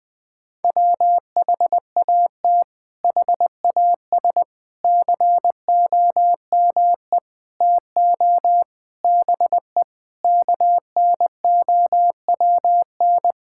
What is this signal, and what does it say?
Morse code, heard as a keyed tone, about 20 wpm, 700 Hz. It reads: WHAT HAS COME TO BE KNOWN
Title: Sets of 8 Words - 20wpm